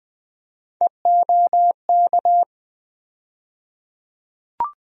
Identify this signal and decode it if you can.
Morse code, heard as a keyed tone, about 20 wpm, 700 Hz. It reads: EOK
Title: USA Call Sign Postfixes - 3 Letters 20wpm